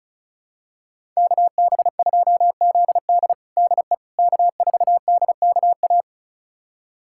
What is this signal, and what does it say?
Morse code, heard as a keyed tone, about 36 wpm, 700 Hz. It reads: KB2ZD DE K4DKA
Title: USA Callsign DE Callsign - 5 Characters - 35wpm